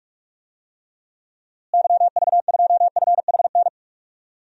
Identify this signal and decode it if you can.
Morse code, heard as a keyed tone, about 46 wpm, 700 Hz. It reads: YV2FHN
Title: DXCC-100 Call Signs - 6 Characters - Repeated 3x - 45wpm